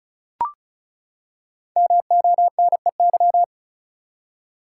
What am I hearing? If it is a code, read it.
Morse code, heard as a keyed tone, about 36 wpm, 700 Hz. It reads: MONEY
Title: Five Letter Words - 35wpm